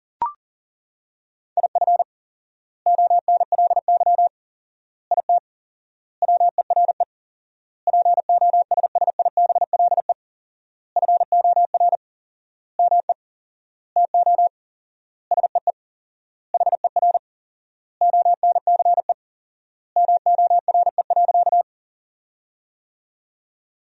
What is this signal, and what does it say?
Morse code, heard as a keyed tone, about 37 wpm, 700 Hz. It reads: IF ONLY IT WERE POSSIBLE FOR ME TO SEE HER ONCE MORE.
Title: 4x Word-Spacing - Sentences from Top 500 Words - 40wpm